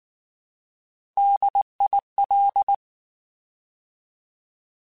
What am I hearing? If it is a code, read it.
Morse code, heard as a keyed tone, about 19 wpm, 800 Hz. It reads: DIL
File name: Alphabetical / Alphabetical mp3s / Ds @18